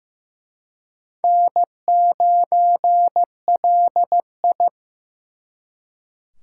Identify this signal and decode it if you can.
Morse code, heard as a keyed tone, about 18 wpm, 700 Hz. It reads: N9LI
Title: Speed Racing - USA 1x2 Call Signs 15wpm